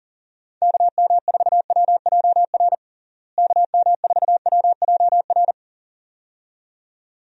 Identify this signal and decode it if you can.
Morse code, heard as a keyed tone, about 37 wpm, 700 Hz. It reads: KM4WJR KM4WJR
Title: USA Call Signs - 6 Characters - Repeated 2x - 40wpm